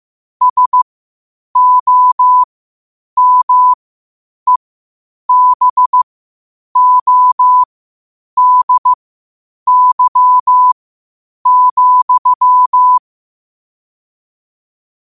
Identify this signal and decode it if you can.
Morse code, heard as a keyed tone, about 15 wpm, spaced as for 8 wpm, 1000 Hz. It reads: SOMEBODY,
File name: qotdfiles / 8wpm